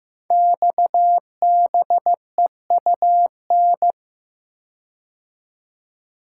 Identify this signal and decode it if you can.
Morse code, heard as a keyed tone, about 15 wpm, 700 Hz. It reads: XBEUN
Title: Code Groups - 5 Letters - 15wpm